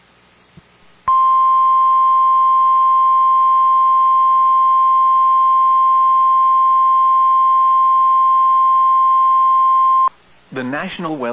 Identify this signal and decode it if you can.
Emergency Alert System (SAME) tones (the 1050 Hz attention tone), with speech. It is heard from 1.0 to 10.2 s.